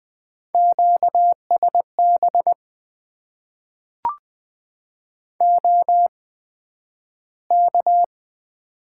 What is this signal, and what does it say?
Morse code, heard as a keyed tone, about 20 wpm, spaced as for 11 wpm, 700 Hz. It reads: QSB OK
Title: Top 90 QSO Elements - 8x Character Spacing - 20wpm